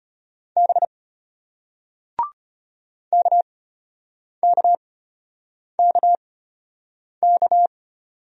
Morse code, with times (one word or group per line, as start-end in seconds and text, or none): 0.6-0.9 B
3.1-7.7 KKKK